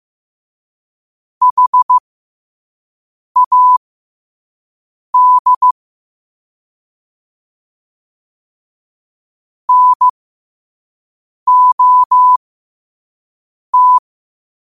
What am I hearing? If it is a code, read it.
Morse code, heard as a keyed tone, about 15 wpm, spaced as for 5 wpm, 1000 Hz. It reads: HAD NOT
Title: 5 WPM morse code quotes for Fri, 01 Mar 2024